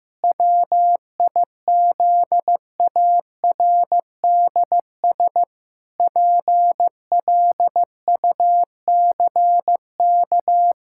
Morse code, heard as a keyed tone, about 16 wpm, 700 Hz. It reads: WIZARDS PLUCK
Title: Pangram Challenge v2 - 15wpm